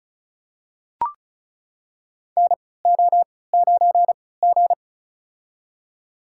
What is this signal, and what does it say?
Morse code, heard as a keyed tone, about 34 wpm, spaced as for 20 wpm, 700 Hz. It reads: NO9G
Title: USA Call Signs - 4 Characters - 3x Character Spacing - 35wpm